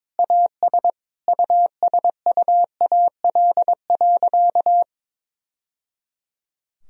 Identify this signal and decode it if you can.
Morse code, heard as a keyed tone, about 22 wpm, 700 Hz. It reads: AS USUAL.
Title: Sentences from Top 1000 Words - Review Part 2 - 22wpm